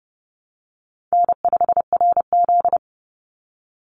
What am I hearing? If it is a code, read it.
Morse code, heard as a keyed tone, about 30 wpm, 700 Hz. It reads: N5RZ
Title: Easy Call Signs Repeated 2x - 30wpm